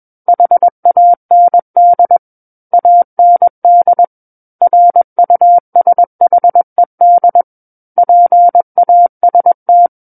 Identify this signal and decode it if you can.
Morse code, heard as a keyed tone, about 21 wpm, 700 Hz. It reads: HAND AND RUSHED PAST
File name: Book One / 12-Chapter 12 - 21 WPM